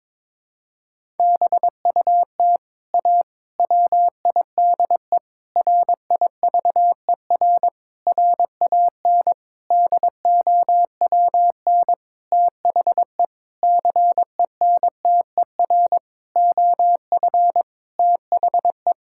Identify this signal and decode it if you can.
Morse code, heard as a keyed tone, about 22 wpm, 700 Hz. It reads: BUT A WIDE RIVER RAN DOWN THE CENTER OF THE
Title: Sentences from Top 1000 Words - Review Part 6 - 22wpm